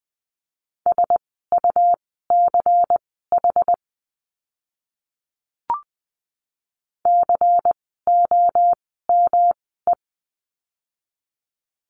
Morse code, heard as a keyed tone, about 20 wpm, spaced as for 14 wpm, 700 Hz. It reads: SUCH COME
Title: 115 Lesson 19f - Letter w - Words 20wpm